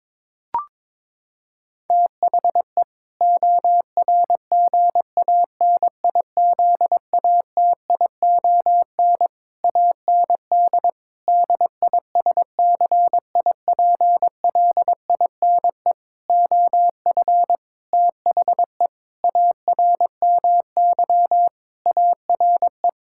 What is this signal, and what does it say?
Morse code, heard as a keyed tone, about 22 wpm, 700 Hz. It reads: THE ORGANIZATION AND DISCIPLINE OF THE ARMY ARE
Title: Sentences from Top 1000 Words - Review Part 6 - 22wpm